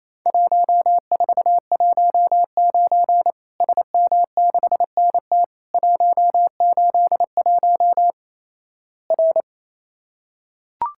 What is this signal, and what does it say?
Morse code, heard as a keyed tone, about 28 wpm, 700 Hz. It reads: 1419 SM6NT 181 R
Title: Worked All Europe DX Contest - QTC Exchange - 28wpm